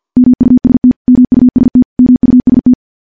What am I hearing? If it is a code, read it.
Morse code, heard as a keyed tone, about 50 wpm, 250 Hz. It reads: MUST MUST MUST
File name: morse / morse-training-50WPM